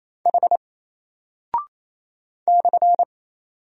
Morse code, heard as a keyed tone, about 27 wpm, 700 Hz. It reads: H /
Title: Single Character (Letter:Number:Punctuation:Prosign) Alt 28wpm